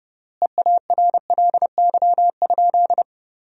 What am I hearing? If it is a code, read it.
Morse code, heard as a keyed tone, about 30 wpm, 700 Hz. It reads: EARLY?
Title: 3x Word-Spacing - Sentences from Top 400 Words - 30wpm